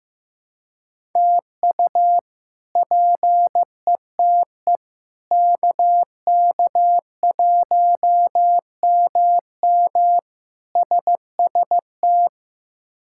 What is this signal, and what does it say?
Morse code, heard as a keyed tone, about 15 wpm, 700 Hz. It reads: TU PETE KK1MM SST
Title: K1USN SST QSO Practice - 15wpm